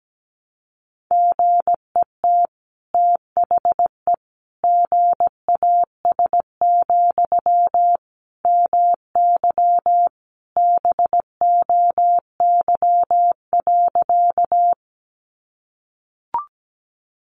Morse code, heard as a keyed tone, about 17 wpm, 700 Hz. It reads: GET THE GAS, MY BOY.
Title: Sentences from Top 1100 Words 17wpm